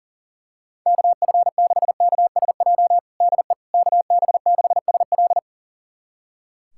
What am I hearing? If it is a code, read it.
Morse code, heard as a keyed tone, about 40 wpm, 700 Hz. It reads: KF6KSJ DE KB6SL
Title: USA Callsign DE Callsign 40wpm